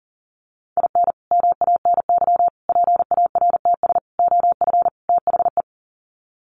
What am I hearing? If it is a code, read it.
Morse code, heard as a keyed tone, about 40 wpm, 700 Hz. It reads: IN MANY PARTS OF THE
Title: Sets of 5 Words Encore 40wpm